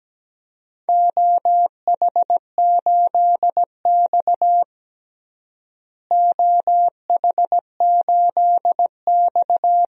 Morse code, heard as a keyed tone, about 17 wpm, 700 Hz. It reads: OH8X OH8X
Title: DXCC-100 Call Signs - 4 Characters - Repeated 2x - 17wpm